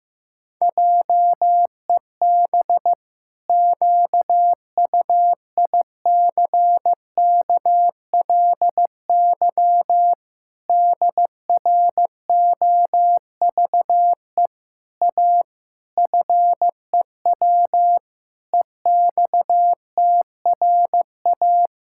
Morse code, heard as a keyed tone, about 14 wpm, 700 Hz. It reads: JEB QUICKLY DROVE A FEW EXTRA
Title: Pangram Challenge v2 - 15wpm